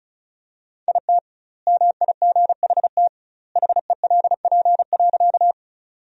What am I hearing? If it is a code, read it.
Morse code, heard as a keyed tone, about 34 wpm, 700 Hz. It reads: IT MIGHT HELP.
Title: 2x Word-Spacing - Sentences from Top 300 Words - 35wpm